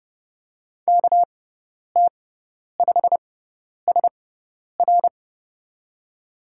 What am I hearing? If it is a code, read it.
Morse code, heard as a keyed tone, about 31 wpm, spaced as for 10 wpm, 700 Hz. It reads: KT5SR
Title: USA Call Signs - 5 Characters - 6x Character Spacing - 30wpm